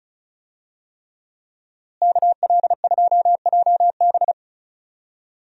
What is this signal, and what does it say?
Morse code, heard as a keyed tone, about 34 wpm, 700 Hz. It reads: KL2JB
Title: USA Call Signs - 5 Characters - Repeated 3x - 35wpm